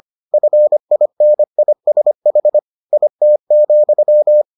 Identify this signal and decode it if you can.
Morse code, heard as a keyed tone, about 25 wpm, 600 Hz. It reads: FINISH IT,